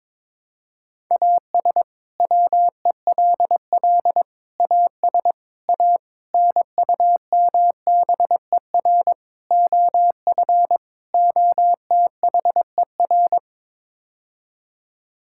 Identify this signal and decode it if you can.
Morse code, heard as a keyed tone, about 22 wpm, 700 Hz. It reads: AS WELL AS A NUMBER OF OTHER
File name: practice / 22wpm / Sets of 7 Words - 22wpm